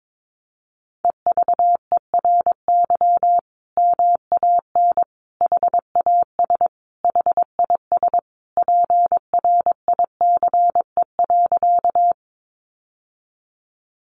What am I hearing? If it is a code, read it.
Morse code, heard as a keyed tone, about 22 wpm, 700 Hz. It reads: EVERY MAN HAS HIS PRICE.
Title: Sentences from Top 600 Words 22wpm